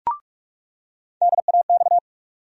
Morse code, heard as a keyed tone, about 46 wpm, 700 Hz. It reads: DAX
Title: USA Call Sign Postfixes - 3 Letters 45wpm